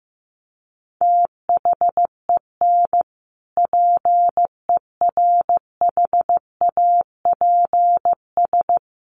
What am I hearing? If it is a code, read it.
Morse code, heard as a keyed tone, about 15 wpm, 700 Hz. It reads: THEN PERHAPS
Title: Sentences from Top 1100 Words 15wpm